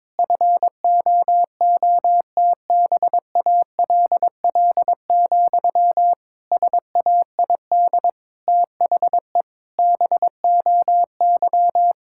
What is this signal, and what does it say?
Morse code, heard as a keyed tone, about 22 wpm, 700 Hz. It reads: FOOTBALL, SAID THE BOY
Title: Sentences from Top 1600 Words 22wpm